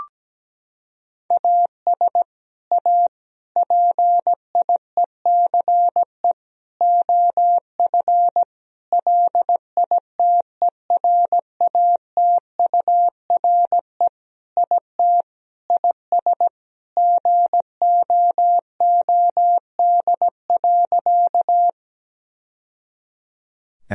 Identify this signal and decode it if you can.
Morse code, heard as a keyed tone, about 17 wpm, 700 Hz. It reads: AS A PIECE OF LITERATURE IT IS GOOD.
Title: Sentences from Top 1000 Words - Review Part 3 - 17wpm